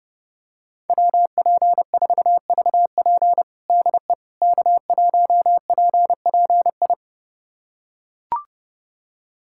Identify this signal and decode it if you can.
Morse code, heard as a keyed tone, about 30 wpm, 700 Hz. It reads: WP4VP DE K1PPI
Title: USA Callsign DE Callsign - 5 Characters - 30wpm